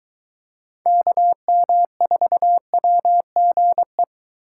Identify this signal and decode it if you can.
Morse code, heard as a keyed tone, about 24 wpm, 700 Hz. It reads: KM4WGE